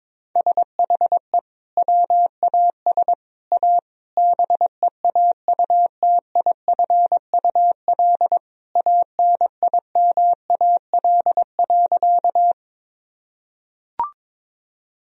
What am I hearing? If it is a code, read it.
Morse code, heard as a keyed tone, about 22 wpm, 700 Hz. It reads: SHE WAS A BEAUTIFUL ANIMAL.
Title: Sentences from Top 1000 Words - Review Part 1 - 22wpm